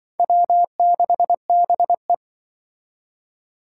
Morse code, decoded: W6BE

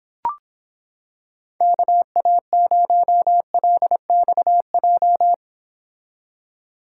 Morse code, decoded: KA0LXJ

